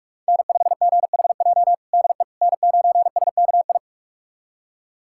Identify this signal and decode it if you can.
Morse code, heard as a keyed tone, about 46 wpm, 700 Hz. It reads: N5GHJ DE N9SKI
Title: USA Callsign DE Callsign - 5 Characters - 45wpm